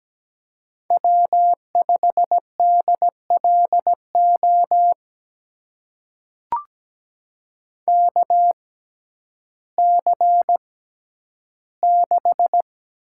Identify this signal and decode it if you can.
Morse code, heard as a keyed tone, about 17 wpm, spaced as for 10 wpm, 700 Hz. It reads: W5DLO KC6
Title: USA Call Signs - 6x Character Spacing - 17wpm